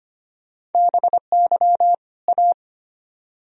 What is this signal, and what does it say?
Morse code, heard as a keyed tone, about 35 wpm, spaced as for 30 wpm, 700 Hz. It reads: BY A
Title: Speed-Racing - Sets of 2 Words 25wpm